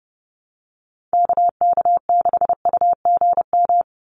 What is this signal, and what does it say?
Morse code, heard as a keyed tone, about 30 wpm, 700 Hz. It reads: KK6UGM
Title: USA Call Signs 30wpm